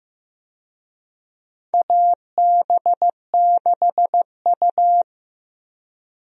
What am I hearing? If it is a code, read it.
Morse code, heard as a keyed tone, about 16 wpm, 700 Hz. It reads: AB6U
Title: USA Call Signs - 4 Characters - Repeated 3x - 15wpm